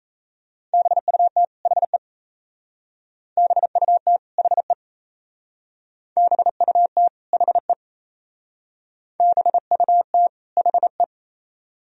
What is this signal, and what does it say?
Morse code, heard as a keyed tone, about 34 wpm, 690 Hz. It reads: BUT HE BUT HE BUT HE BUT HE
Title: Speed-Racing - Sets of 2 Words 28wpm